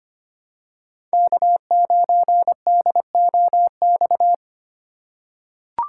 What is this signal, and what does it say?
Morse code, heard as a keyed tone, about 25 wpm, 700 Hz. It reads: K9DOX
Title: USA Call Signs - 5 Characters - 3x Character Spacing - 25wpm